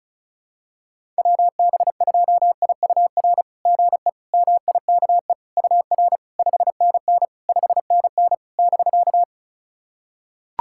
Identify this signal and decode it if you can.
Morse code, heard as a keyed tone, about 36 wpm, 690 Hz. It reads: WB2IUR GE MIKE UR 5NN 5NN <BK>
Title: SOTA QSO Practice 35wpm